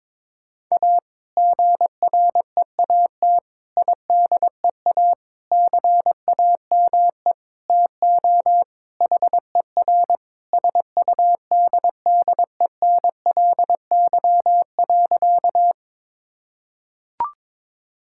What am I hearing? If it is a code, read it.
Morse code, heard as a keyed tone, about 22 wpm, 700 Hz. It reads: A GREAT IDEA CAME TO HER SUDDENLY.
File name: practice / 22wpm / Sentences with 2-Word Phrases - 22wpm